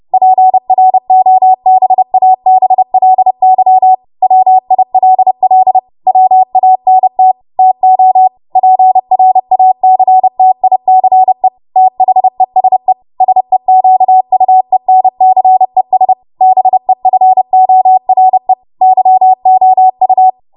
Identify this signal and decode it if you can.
Morse code, heard as a keyed tone, about 30 wpm, 750 Hz. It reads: PROBABLY WILL WANT TO PRACTICE THESE SEQUENCES BEFORE YOU
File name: Morse / 30WPM